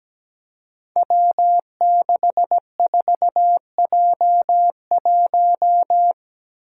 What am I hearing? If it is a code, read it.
Morse code, heard as a keyed tone, about 17 wpm, 700 Hz. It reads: W64J1
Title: Five Challenging Letter-Number Combinations 17wpm